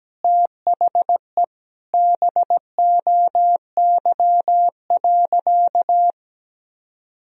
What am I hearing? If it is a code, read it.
Morse code, heard as a keyed tone, about 17 wpm, 700 Hz. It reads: THE BOY.
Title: Sentences from Top 1000 Words - Review Part 4 - 17wpm